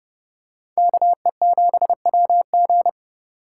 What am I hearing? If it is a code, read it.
Morse code, heard as a keyed tone, about 27 wpm, 700 Hz. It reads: KE7WG